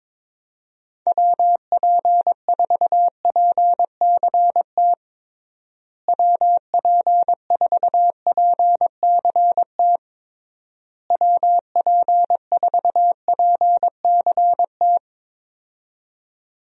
Morse code, heard as a keyed tone, about 22 wpm, 700 Hz. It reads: WP4PCT WP4PCT WP4PCT
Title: USA Call Signs - 6 Characters - Repeated 3x - 22wpm